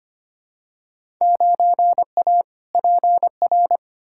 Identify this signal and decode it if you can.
Morse code, heard as a keyed tone, about 25 wpm, 700 Hz. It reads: 9A PR
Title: Field Day Exchanges Encore - 25wpm